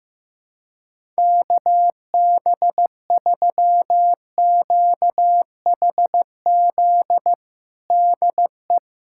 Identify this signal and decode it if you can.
Morse code, heard as a keyed tone, about 15 wpm, 700 Hz. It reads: KB3QHZ DE
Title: USA Callsign DE Callsign - 6 Characters - 15wpm